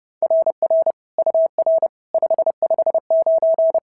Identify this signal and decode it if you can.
Morse code, heard as a keyed tone, about 26 wpm, 650 Hz. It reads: RR UR 559
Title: SOTA QSO Practice 30wpm